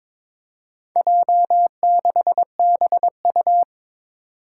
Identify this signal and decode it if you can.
Morse code, heard as a keyed tone, about 21 wpm, 700 Hz. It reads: J6BU
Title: Four Challenging Letter-Number Combinations 22wpm